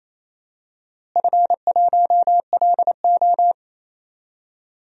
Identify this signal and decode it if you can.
Morse code, heard as a keyed tone, about 28 wpm, 700 Hz. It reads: F1LO
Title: DXCC-100 Call Signs - 4 Characters - Repeated 2x - 28wpm